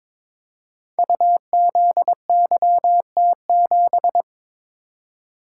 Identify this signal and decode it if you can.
Morse code, heard as a keyed tone, about 22 wpm, 700 Hz. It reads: UZYT7
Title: Code Groups - 5 Letter:Numbers - 22wpm